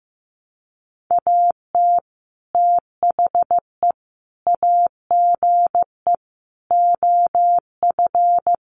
Morse code, heard as a keyed tone, about 15 wpm, 700 Hz. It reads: AT THE AGE OF
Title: Sets of 4 Words 15wpm